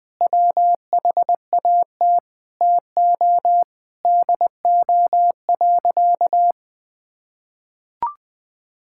Morse code, heard as a keyed tone, about 20 wpm, 700 Hz. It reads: WHAT TO DO.
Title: Sentences with 3-Word Phrases - 20wpm